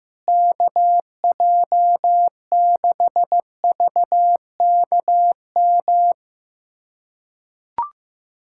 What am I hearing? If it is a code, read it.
Morse code, heard as a keyed tone, about 15 wpm, 700 Hz. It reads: KJ6VKM
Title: USA Callsign DE Callsign - 6 Characters - 15wpm